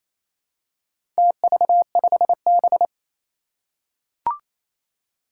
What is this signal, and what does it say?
Morse code, heard as a keyed tone, about 27 wpm, 700 Hz. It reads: TV5B
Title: Four Challenging Letter-Number Combinations 28wpm